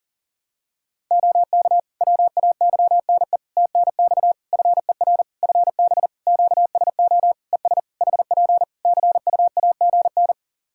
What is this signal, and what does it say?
Morse code, heard as a keyed tone, about 41 wpm, 700 Hz. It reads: OK WAYNE TNX FER FB QSO ES HP CUAGN
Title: Practice QSO - Exchange 4 - Ending - 40wpm